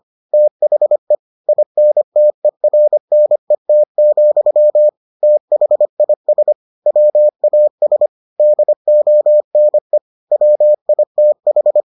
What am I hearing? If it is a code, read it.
Morse code, heard as a keyed tone, about 25 wpm, 600 Hz. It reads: THE INTERNET, THIS WAS DONE WITH